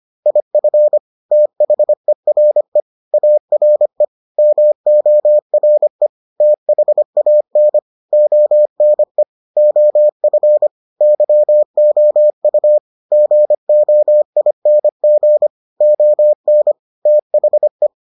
Morse code, heard as a keyed tone, about 25 wpm, 600 Hz. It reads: IF THERE ARE MORE THAN ONE OF YOU GOING ON THE